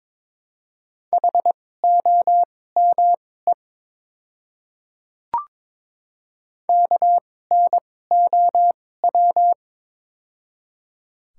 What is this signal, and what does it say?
Morse code, heard as a keyed tone, about 22 wpm, spaced as for 16 wpm, 700 Hz. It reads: HOME KNOW
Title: 2x Character Spacing - Top 200 Words - 22wpm